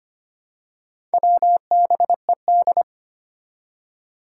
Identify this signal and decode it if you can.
Morse code, heard as a keyed tone, about 25 wpm, 700 Hz. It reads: WBED
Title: Four Challenging Letter-Number Combinations 25wpm